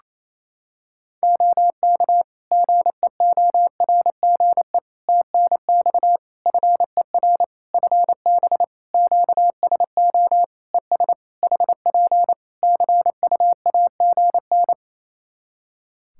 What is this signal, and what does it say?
Morse code, heard as a keyed tone, about 28 wpm, 700 Hz. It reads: OK GEORGE TNX FER FB QSO ES HP CUAGN
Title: K1USN SST QSO Practice 28wpm